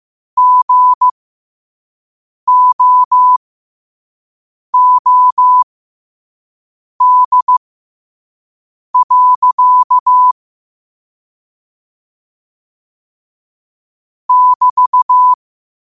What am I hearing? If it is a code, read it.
Morse code, heard as a keyed tone, about 15 wpm, spaced as for 5 wpm, 1000 Hz. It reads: GOOD. <BT>
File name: qotdfiles / 5wpm